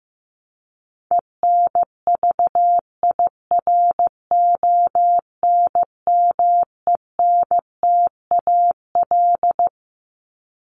Morse code, heard as a keyed tone, about 15 wpm, 700 Hz. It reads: ENVIRONMENTAL
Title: Top 1001-1250 Words 15wpm